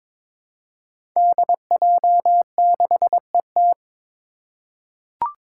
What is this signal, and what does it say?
Morse code, heard as a keyed tone, about 22 wpm, 700 Hz. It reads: DJ6ET
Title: Five Challenging Letter-Number Combinations 22wpm